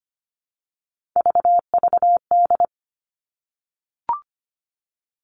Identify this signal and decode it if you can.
Morse code, heard as a keyed tone, about 25 wpm, 700 Hz. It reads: VVD
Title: Three Challenging Letter-Number Combinations 25wpm